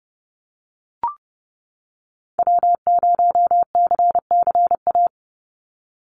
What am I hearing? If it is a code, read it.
Morse code, heard as a keyed tone, about 30 wpm, 700 Hz. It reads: W0CCA
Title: NA SOTA Activator Callsigns 30wpm